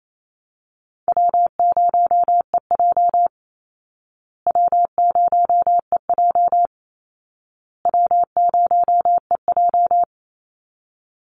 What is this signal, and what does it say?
Morse code, heard as a keyed tone, about 28 wpm, 700 Hz. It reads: W0EJ W0EJ W0EJ
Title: Easy Call Signs Repeated 3x - 28wpm